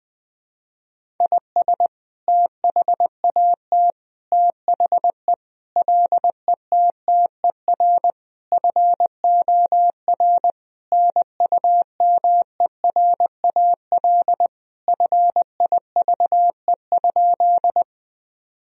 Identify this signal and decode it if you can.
Morse code, heard as a keyed tone, about 20 wpm, 700 Hz. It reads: IS THAT THE LETTER FOR NUMERAL FIVE?
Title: Sentences from Top 1000 Words - Review Part 2 - 20wpm